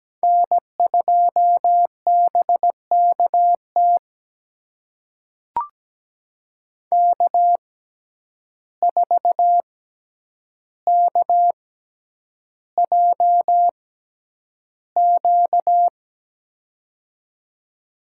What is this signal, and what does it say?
Morse code, heard as a keyed tone, about 17 wpm, spaced as for 9 wpm, 700 Hz. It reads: N2BKT K4KJQ
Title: USA Call Signs - 5 Characters - 6x Character Spacing - 17wpm